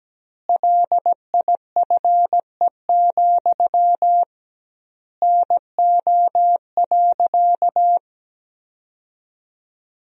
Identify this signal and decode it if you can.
Morse code, heard as a keyed tone, about 17 wpm, 700 Hz. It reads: LIFE, NO.
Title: 2x Word-Spacing - Sentences from Top 300 Words - 17wpm